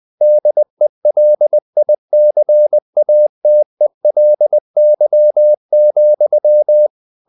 Morse code, heard as a keyed tone, about 20 wpm, 600 Hz. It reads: DELICATELY,
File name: the old man and the sea / the old man and the sea part 1 20wpm